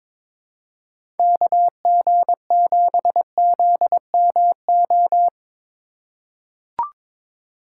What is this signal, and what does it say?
Morse code, heard as a keyed tone, about 22 wpm, 700 Hz. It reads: KG7ZMO
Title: USA Call Signs - 6 Characters - 4x Character Spacing - 22wpm